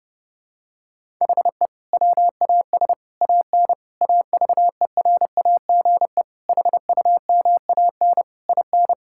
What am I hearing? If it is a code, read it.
Morse code, heard as a keyed tone, about 32 wpm, 700 Hz. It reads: HE WAS AN AVERAGE HUMAN IN
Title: Sentences from Top 1000 Words - Review Part 6 - 30wpm